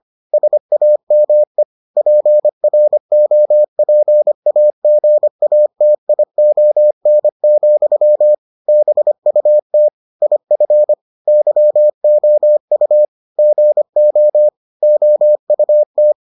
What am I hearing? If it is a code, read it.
Morse code, heard as a keyed tone, about 25 wpm, 600 Hz. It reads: SAME PROPAGATION, BUT IF YOU GO OUT